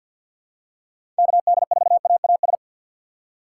Morse code, heard as a keyed tone, about 47 wpm, 700 Hz. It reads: KD4AAS
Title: USA Call Signs - 2x Character Spacing - 50wpm